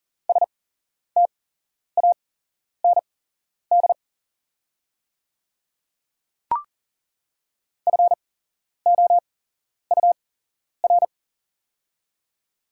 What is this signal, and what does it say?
Morse code, heard as a keyed tone, about 39 wpm, spaced as for 11 wpm, 700 Hz. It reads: STAND FOUR
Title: 8x Character Spacing - Top 300 Words - 40wpm